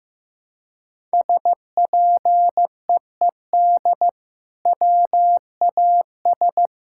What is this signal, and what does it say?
Morse code, heard as a keyed tone, about 15 wpm, 700 Hz. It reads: SPEED WAS
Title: Sentences from Top 1000 Words - Review Part 5 - 15wpm